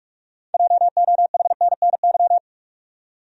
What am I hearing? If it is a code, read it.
Morse code, heard as a keyed tone, about 42 wpm, 700 Hz. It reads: JOHNNY
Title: USA Common Names - 45wpm